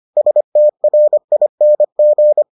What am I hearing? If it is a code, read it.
Morse code, heard as a keyed tone, about 25 wpm, 600 Hz. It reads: STRING